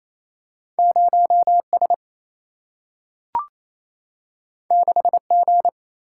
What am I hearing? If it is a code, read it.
Morse code, heard as a keyed tone, about 28 wpm, 700 Hz. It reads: 0S 6G
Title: Single-Letter Single-Number 28wpm